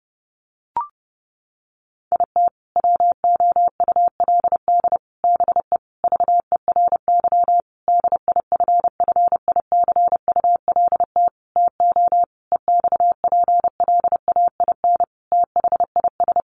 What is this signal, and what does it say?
Morse code, heard as a keyed tone, about 28 wpm, 700 Hz. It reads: IT WOULD BE VERY DIFFICULT TO EXPLAIN THIS
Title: Sentences from Top 1100 Words 30wpm